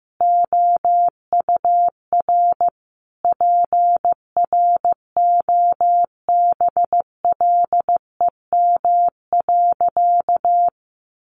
Morse code, heard as a keyed tone, about 15 wpm, 700 Hz. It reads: OUR PROBLEM.
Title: Sentences from Top 1300 Words 15wpm